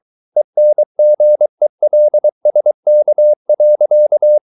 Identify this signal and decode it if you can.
Morse code, heard as a keyed tone, about 23 wpm, 600 Hz. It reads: ENGELSK.
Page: Ferie 23 wpm